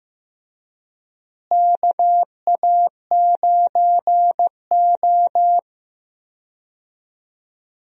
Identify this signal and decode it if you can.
Morse code, heard as a keyed tone, about 15 wpm, 700 Hz. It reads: KA9O
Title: USA Call Signs - 4 Characters - Repeated 2x - 15wpm